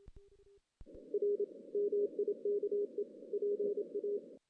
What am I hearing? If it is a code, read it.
Morse code, heard as a keyed tone, about 27 wpm, 410 Hz. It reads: R MIKE PA
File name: 2016 JAN CW NAQP / 2016 Jan naqp k5ot - 0-6